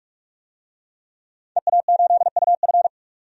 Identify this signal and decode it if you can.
Morse code, heard as a keyed tone, about 46 wpm, 700 Hz. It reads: EA8UF
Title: DXCC-100 Call Signs - 5 Characters - 45wpm